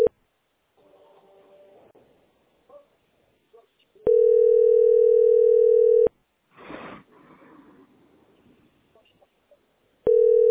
Ringback tone (North America), 440 + 480 Hz.